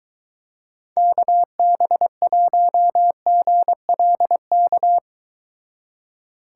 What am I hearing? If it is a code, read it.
Morse code, heard as a keyed tone, about 23 wpm, 700 Hz. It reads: KB1GLK